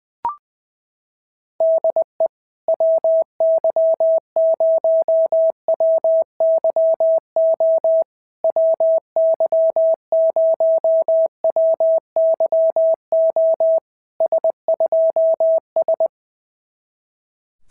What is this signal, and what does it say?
Morse code, heard as a keyed tone, about 19 wpm, 650 Hz. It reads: DE WY0WYO WY0WYO S2S
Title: SOTA QSO Practice 20wpm